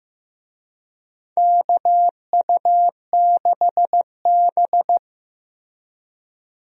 Morse code, heard as a keyed tone, about 16 wpm, 700 Hz. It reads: KU6B
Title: USA Call Signs - 4 Characters - Repeated 3x - 15wpm